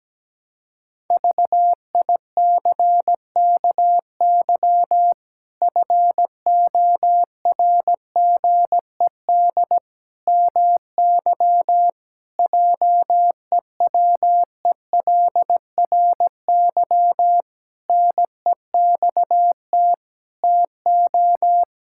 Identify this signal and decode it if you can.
Morse code, heard as a keyed tone, about 17 wpm, 700 Hz. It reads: VICKY FORGED MY JEWELRY NEXT TO
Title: Pangram Challenge v2 - 17wpm